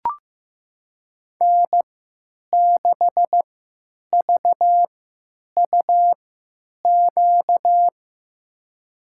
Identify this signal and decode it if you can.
Morse code, heard as a keyed tone, about 15 wpm, spaced as for 9 wpm, 700 Hz. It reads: N6VUQ
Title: USA Call Signs - 5 Characters - 3x Character Spacing - 15wpm